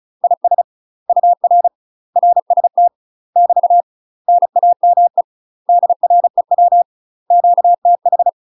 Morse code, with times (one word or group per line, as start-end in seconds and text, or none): 0.2-0.6 IS
1.1-1.7 UR
2.2-2.9 RST
3.4-3.8 <BT>
4.3-5.2 NAME
5.7-6.8 DREW
7.3-8.3 QTH